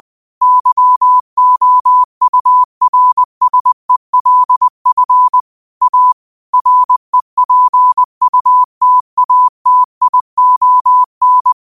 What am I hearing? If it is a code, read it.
Morse code, heard as a keyed tone, about 20 wpm, 1000 Hz. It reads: YOURSELF A REPUTATION
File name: qotdfiles / 20wpm